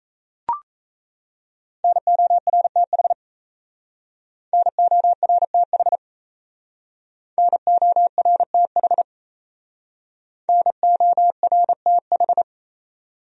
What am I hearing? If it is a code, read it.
Morse code, heard as a keyed tone, about 35 wpm, 700 Hz. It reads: NORTH NORTH NORTH NORTH
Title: Speed-Racing Top 500 Words 28wpm